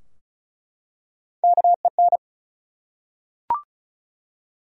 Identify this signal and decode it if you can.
Morse code, heard as a keyed tone, about 34 wpm, 700 Hz. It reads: KEN
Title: USA Common Names - Max 4 Letters - 35wpm